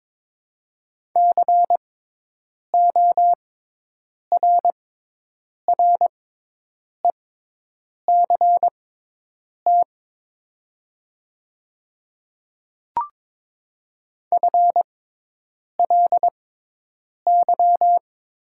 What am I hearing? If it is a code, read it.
Morse code, heard as a keyed tone, about 21 wpm, spaced as for 8 wpm, 700 Hz. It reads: CORRECT FLY
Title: 6x Character Spacing - Top 500 Words - 22wpm